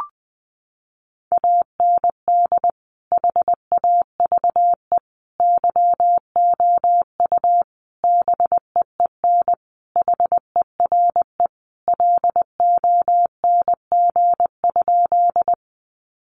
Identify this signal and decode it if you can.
Morse code, heard as a keyed tone, about 20 wpm, 700 Hz. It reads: AND HAVE YOU BEEN HERE LONG?
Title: Sentences from Top 200 Words 20wpm